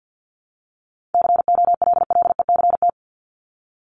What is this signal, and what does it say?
Morse code, heard as a keyed tone, about 48 wpm, 700 Hz. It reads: COLLECT
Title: Top 1501-1750 Words 50wpm